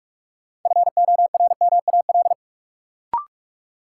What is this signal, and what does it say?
Morse code, heard as a keyed tone, about 41 wpm, 700 Hz. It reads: FORMAL
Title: Sentences from Top 1300 Words - New Word Drill Alt - 45wpm